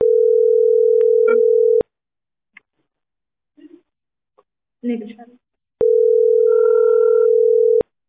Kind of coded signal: telephone call progress tone